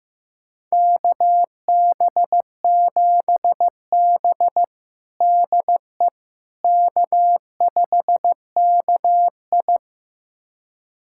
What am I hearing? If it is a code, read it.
Morse code, heard as a keyed tone, about 15 wpm, 700 Hz. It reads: KB7B DE K5KI
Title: USA Callsign DE Callsign - 4 Characters - 15wpm